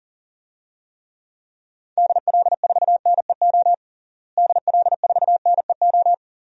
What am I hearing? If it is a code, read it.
Morse code, heard as a keyed tone, about 40 wpm, 700 Hz. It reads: DL4NEO DL4NEO
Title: DXCC-100 Call Signs - 6 Characters - Repeated 2x - 40wpm